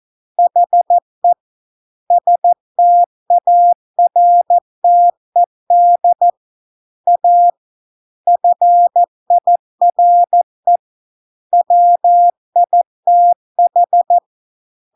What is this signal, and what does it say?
Morse code, heard as a keyed tone, about 14 wpm, 700 Hz. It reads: HE STARTED A FIRE WITH